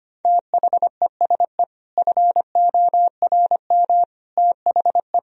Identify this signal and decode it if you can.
Morse code, heard as a keyed tone, about 25 wpm, 700 Hz. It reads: THESE FORM THE
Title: Sentences from Top 1000 Words - Review Part 1 - 25wpm